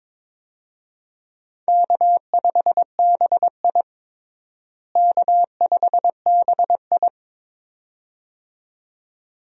Morse code, heard as a keyed tone, about 22 wpm, 700 Hz. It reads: K5BI K5BI
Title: USA Call Signs - 4 Characters - Repeated 2x - 22wpm